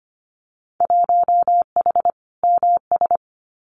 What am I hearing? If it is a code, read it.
Morse code, heard as a keyed tone, about 25 wpm, 700 Hz. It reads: 1H MS
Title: Winter Field Day Exchanges 25wpm